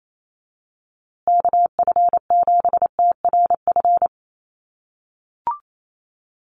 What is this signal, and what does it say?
Morse code, heard as a keyed tone, about 28 wpm, 700 Hz. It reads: KF7TRF
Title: USA Call Signs Repeated 2x 28wpm